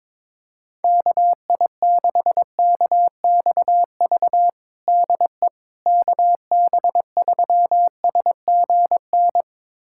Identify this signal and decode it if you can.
Morse code, heard as a keyed tone, about 21 wpm, 700 Hz. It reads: KI6KXV DE KB3SGN
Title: USA Callsign DE Callsign 22wpm